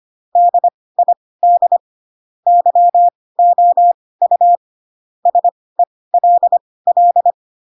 Morse code, heard as a keyed tone, about 25 wpm, spaced as for 18 wpm, 700 Hz. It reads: DID YOU SELL